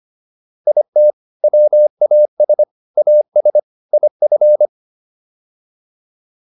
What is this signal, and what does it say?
Morse code, heard as a keyed tone, about 25 wpm, 600 Hz. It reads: IT WAS AS IF